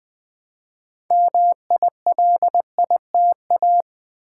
Morse code, heard as a keyed tone, about 20 wpm, 700 Hz. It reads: MILITA
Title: Build A Word - 3 Letters At A Time - 8-Letter Words - 20wpm